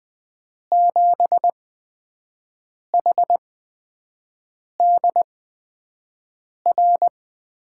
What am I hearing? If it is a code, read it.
Morse code, heard as a keyed tone, about 20 wpm, spaced as for 5 wpm, 700 Hz. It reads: 7HDR
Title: USA Call Signs - 6 Characters - 8x Character Spacing - 20wpm